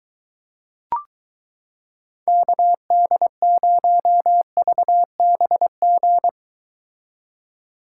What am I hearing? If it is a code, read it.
Morse code, heard as a keyed tone, about 23 wpm, 700 Hz. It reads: KD0VBG